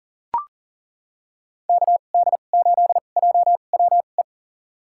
Morse code, heard as a keyed tone, about 39 wpm, spaced as for 29 wpm, 700 Hz. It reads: KD8JWE
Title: USA Call Signs - 2x Character Spacing - 40wpm